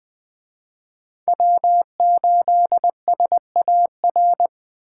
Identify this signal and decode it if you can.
Morse code, heard as a keyed tone, about 20 wpm, 700 Hz. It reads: W8SAR
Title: Speed Racing - USA 1x3 Call Signs 17wpm